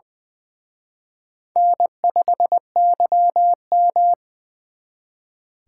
Morse code, heard as a keyed tone, about 23 wpm, 700 Hz. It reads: N5YM
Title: Speed Racing - USA 1x2 Call Signs 20wpm